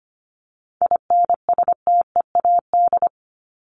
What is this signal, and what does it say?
Morse code, heard as a keyed tone, about 25 wpm, 700 Hz. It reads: INSTEAD
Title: Top 501-750 Words 25wpm